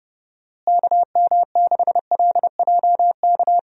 Morse code, heard as a keyed tone, about 29 wpm, 700 Hz. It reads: KM6LJK
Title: USA Call Signs - 6 Characters - Repeated 2x - 30wpm